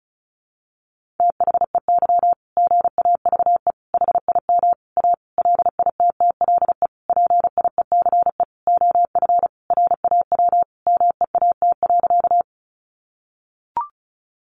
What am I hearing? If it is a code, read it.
Morse code, heard as a keyed tone, about 34 wpm, 700 Hz. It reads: THEY GAVE HIM A LITTLE PIECE OF RAW MEAT.
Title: Sentences from Top 1500 Words 35wpm